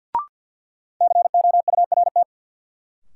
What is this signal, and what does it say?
Morse code, heard as a keyed tone, about 52 wpm, 700 Hz. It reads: COURT
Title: Sentences from Top 1600 Words - New Word Drill Alt - 50wpm